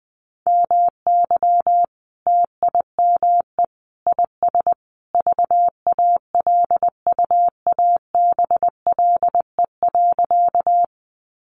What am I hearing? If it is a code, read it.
Morse code, heard as a keyed tone, about 19 wpm, 700 Hz. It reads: MY TIME IS VALUABLE.
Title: Sentences from Top 1100 Words 20wpm